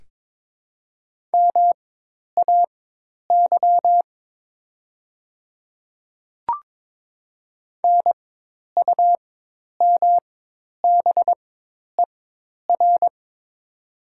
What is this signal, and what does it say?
Morse code, heard as a keyed tone, about 22 wpm, spaced as for 10 wpm, 700 Hz. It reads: MAY NUMBER
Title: 4x Character Spacing - Top 100 Words - 22wpm